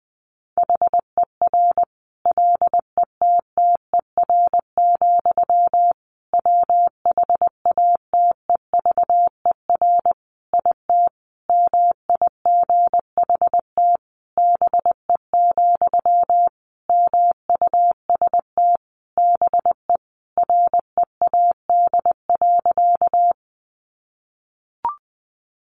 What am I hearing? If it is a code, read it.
Morse code, heard as a keyed tone, about 20 wpm, 700 Hz. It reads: HER LETTER, WHATEVER IT MIGHT BE, MUST BE READ.
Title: Sentences from Top 1100 Words 20wpm